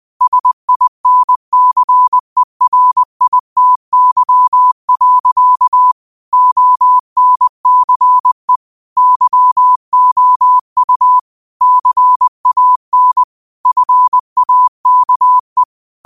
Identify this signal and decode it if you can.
Morse code, heard as a keyed tone, about 20 wpm, 1000 Hz. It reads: SINCERITY. ONCE YOU CAN FAKE
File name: qotdfiles / 20wpm